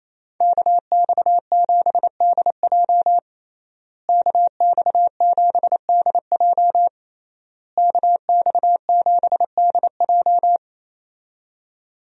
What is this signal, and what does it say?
Morse code, heard as a keyed tone, about 28 wpm, 700 Hz. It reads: KX7DJ KX7DJ KX7DJ
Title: USA Call Signs - 5 Characters - Repeated 3x - 28wpm